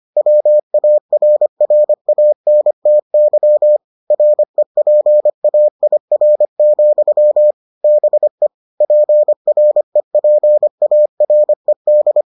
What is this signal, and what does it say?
Morse code, heard as a keyed tone, about 25 wpm, 600 Hz. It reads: WARRANTY REPAIR, BE PREPARED